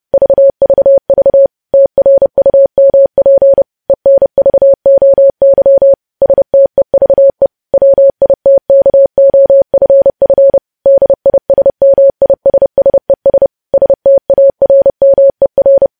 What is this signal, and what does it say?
Morse code, heard as a keyed tone, about 30 wpm, 550 Hz. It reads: VVV TRUMP ENVOY STEVE WITKOFF DISMISSES STARMER
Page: news headlines in morse code at 30 wpm